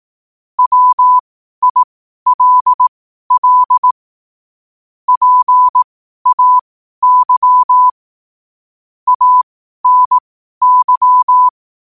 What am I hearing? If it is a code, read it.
Morse code, heard as a keyed tone, about 18 wpm, spaced as for 12 wpm, 1000 Hz. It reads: WILL PAY ANY
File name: qotdfiles / 12wpm